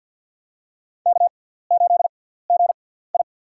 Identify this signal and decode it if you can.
Morse code, heard as a keyed tone, about 48 wpm, spaced as for 17 wpm, 690 Hz. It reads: K8GI
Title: USA Call Signs - 4 Characters - 6x Character Spacing - 50wpm